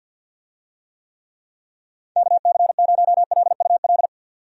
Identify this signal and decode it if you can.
Morse code, heard as a keyed tone, about 47 wpm, 700 Hz. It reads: KC0LUL
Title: USA Call Signs - 6 Characters - 50wpm